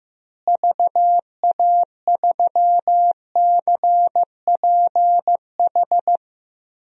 Morse code, heard as a keyed tone, about 15 wpm, 700 Hz. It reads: VA3CPH
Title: DXCC-100 Call Signs - 6 Characters - Repeated 3x - 15wpm